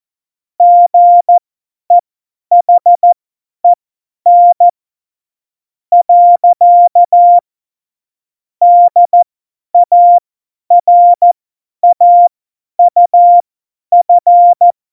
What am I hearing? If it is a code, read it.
Morse code, heard as a keyed tone, about 14 wpm, spaced as for 9 wpm, 700 Hz. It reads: GEHEN. DARAUF
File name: Morsekurs / Witze / Witz 25. September 2018, 50 F70